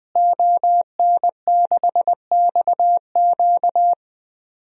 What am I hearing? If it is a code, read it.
Morse code, heard as a keyed tone, about 20 wpm, 700 Hz. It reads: ON6XQ
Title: DXCC-100 Call Signs - 5 Characters - Repeated 2x - 20wpm